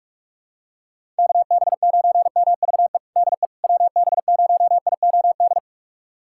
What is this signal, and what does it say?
Morse code, heard as a keyed tone, about 42 wpm, 690 Hz. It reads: KB9MVE DE WB0IOD
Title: USA Callsign DE Callsign 45wpm